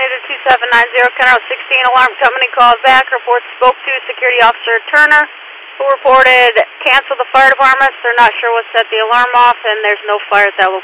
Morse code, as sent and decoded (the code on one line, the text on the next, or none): none